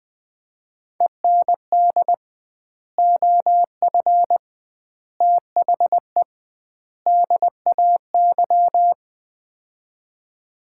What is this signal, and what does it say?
Morse code, heard as a keyed tone, about 20 wpm, 700 Hz. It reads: END OF THE DAY
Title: 2x Word-Spacing - Sets of 4 Words - 20wpm